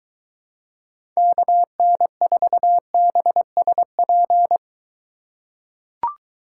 Morse code, heard as a keyed tone, about 22 wpm, 700 Hz. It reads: KN4BSP